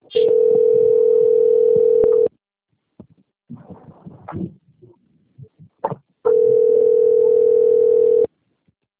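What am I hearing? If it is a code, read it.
Ringback tone (North America), 440 + 480 Hz.
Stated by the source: Outbound phone call recorded on a PBX system labelled Rio Claro - SP.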